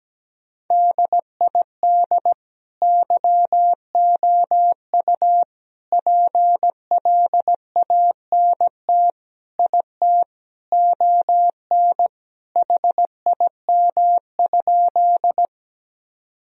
Morse code, heard as a keyed tone, about 17 wpm, 700 Hz. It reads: DID YOU PLANT IT ON HIM?
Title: Sentences from Top 1000 Words - Review Part 1 - 17wpm